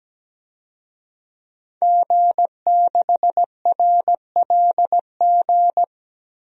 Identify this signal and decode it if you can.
Morse code, heard as a keyed tone, about 17 wpm, 700 Hz. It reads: G6RLG
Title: DXCC-100 Call Signs - 5 Characters - Repeated 3x - 17wpm